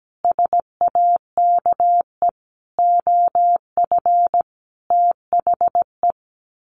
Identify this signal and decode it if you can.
Morse code, heard as a keyed tone, about 17 wpm, 700 Hz. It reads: SAKE OF THE
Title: Sets of 5 Words Encore 17wpm